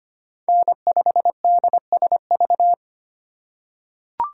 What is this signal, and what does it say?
Morse code, heard as a keyed tone, about 25 wpm, 700 Hz. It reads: N5DSV
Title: USA Call Signs - 5 Characters - 3x Character Spacing - 25wpm